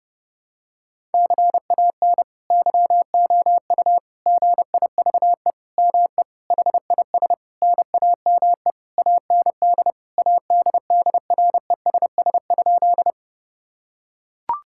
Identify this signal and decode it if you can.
Morse code, heard as a keyed tone, about 30 wpm, 700 Hz. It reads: CAN YOU GIVE ME HIS NAME AND ADDRESS?
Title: Sentences from Top 1000 Words - Review Part 6 - 30wpm